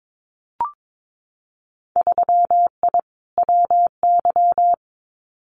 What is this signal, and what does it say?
Morse code, heard as a keyed tone, about 22 wpm, 700 Hz. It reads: 3I WY
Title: Winter Field Day Exchanges Encore 22wpm